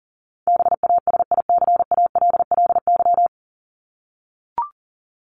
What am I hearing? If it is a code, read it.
Morse code, heard as a keyed tone, about 40 wpm, 700 Hz. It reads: BASICALLY
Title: Top 1251-1500 Words 40wpm